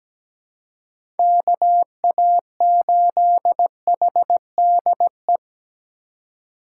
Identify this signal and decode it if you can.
Morse code, heard as a keyed tone, about 17 wpm, 700 Hz. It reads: KA8HDE
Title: K1USN SST QSO Practice - 17wpm